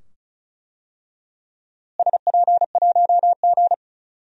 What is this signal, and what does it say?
Morse code, heard as a keyed tone, about 36 wpm, 700 Hz. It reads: SP1G
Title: DXCC-100 Call Signs - 4 Characters - Repeated 2x - 35wpm